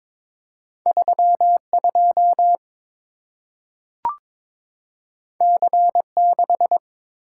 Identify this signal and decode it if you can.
Morse code, heard as a keyed tone, about 22 wpm, 700 Hz. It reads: 32 C6
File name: practice / 22wpm / Two Letter-Number Combinations - 22wpm